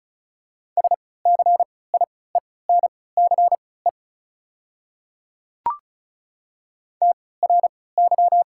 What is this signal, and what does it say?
Morse code, heard as a keyed tone, about 36 wpm, spaced as for 20 wpm, 700 Hz. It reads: SCIENCE TRY
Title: 3x Character Spacing - Top 500 Words - 35wpm